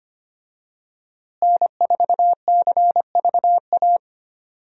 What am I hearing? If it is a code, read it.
Morse code, heard as a keyed tone, about 25 wpm, 700 Hz. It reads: N4CVA
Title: USA Call Signs - 5 Characters - Repeated 3x - 25wpm